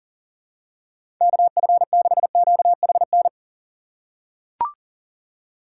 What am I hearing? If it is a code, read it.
Morse code, heard as a keyed tone, about 40 wpm, 700 Hz. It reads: KF6QHN
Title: USA Call Signs - 6 Characters - 6x Character Spacing - 40wpm